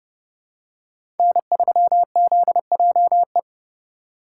Morse code, heard as a keyed tone, about 30 wpm, 700 Hz. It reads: N3ZJE
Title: USA Call Signs - 8x Character Spacing - 30wpm